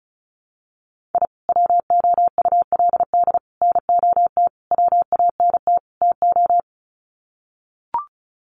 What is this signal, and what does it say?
Morse code, heard as a keyed tone, about 34 wpm, 700 Hz. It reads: I WOULD NOT WANT TO
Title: Sets of 5 Words 35wpm